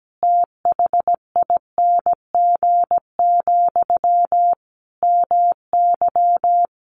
Morse code, heard as a keyed tone, about 17 wpm, 700 Hz. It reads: THING, MY
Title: Sentences from Top 1200 Words 17wpm